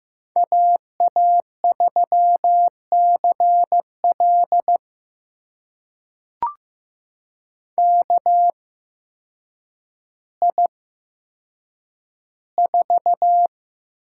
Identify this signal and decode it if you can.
Morse code, heard as a keyed tone, about 15 wpm, spaced as for 8 wpm, 700 Hz. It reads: AA3CL KI4
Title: USA Call Signs - 8x Character Spacing - 15wpm